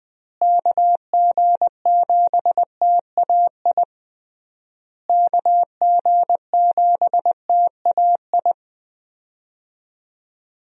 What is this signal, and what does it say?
Morse code, heard as a keyed tone, about 20 wpm, 700 Hz. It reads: KG7TAI KG7TAI
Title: USA Call Signs - 6 Characters - Repeated 3x - 20wpm